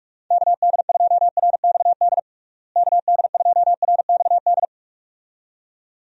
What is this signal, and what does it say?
Morse code, heard as a keyed tone, about 42 wpm, 700 Hz. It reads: KD2RXD KD2RXD
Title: USA Call Signs - 6 Characters - Repeated 2x - 45wpm